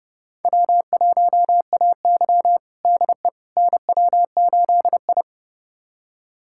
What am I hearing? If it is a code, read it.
Morse code, heard as a keyed tone, about 32 wpm, 700 Hz. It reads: W1AY DE NW8I
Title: USA Callsign DE Callsign - 4 Characters - 30wpm